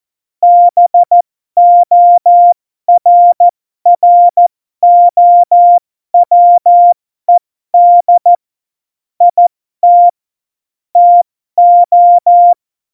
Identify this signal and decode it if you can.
Morse code, heard as a keyed tone, about 14 wpm, spaced as for 12 wpm, 700 Hz. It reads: BORROWED IT TO
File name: Morsekurs / Witze